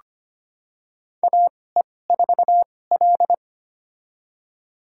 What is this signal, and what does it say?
Morse code, heard as a keyed tone, about 25 wpm, spaced as for 18 wpm, 700 Hz. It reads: AE4L
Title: USA Call Signs - 2x Character Spacing - 25wpm